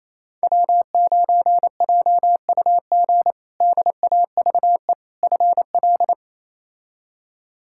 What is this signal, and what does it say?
Morse code, heard as a keyed tone, about 28 wpm, 700 Hz. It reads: W9JUG DAVE FL
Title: K1USN SST QSO Practice - 28wpm